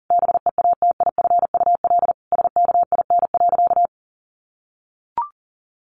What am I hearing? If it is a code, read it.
Morse code, heard as a keyed tone, about 40 wpm, 700 Hz. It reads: BEATIFUL SKIN.
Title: Sentences from Top 1300 Words 40wpm